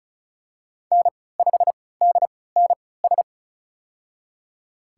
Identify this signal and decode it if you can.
Morse code, heard as a keyed tone, about 34 wpm, spaced as for 20 wpm, 700 Hz. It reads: N5DNS
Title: USA Call Signs - 3x Character Spacing - 35wpm